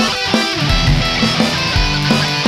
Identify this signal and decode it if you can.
There is no code.